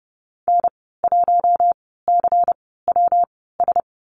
Morse code, heard as a keyed tone, about 30 wpm, spaced as for 17 wpm, 700 Hz. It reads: N1CWS
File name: Lesson 19g - Letter w - Call Signs 30wpm